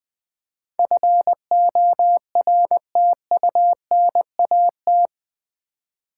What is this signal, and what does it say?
Morse code, heard as a keyed tone, about 20 wpm, 700 Hz. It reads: FORTUNAT
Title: Build A Word - 4 Letters At A Time - 11-Letter Words - 20wpm